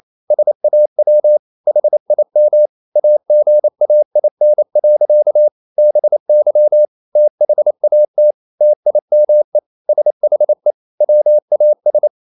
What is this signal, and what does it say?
Morse code, heard as a keyed tone, about 28 wpm, 600 Hz. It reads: SAW HIM AGAIN. BY THAT TIME SHE WAS
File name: Never 0027wpm